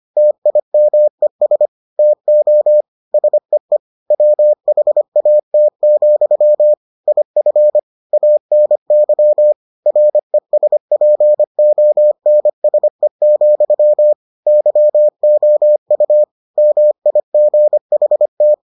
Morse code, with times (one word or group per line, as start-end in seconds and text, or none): 0.2-1.7 TIMES
2.0-2.8 TO
3.1-3.8 SEE
4.1-6.7 WHAT,
7.1-7.8 IF
8.1-9.5 ANY
9.9-14.1 RESPONSE,
14.5-16.2 YOU
16.6-18.5 MIGHT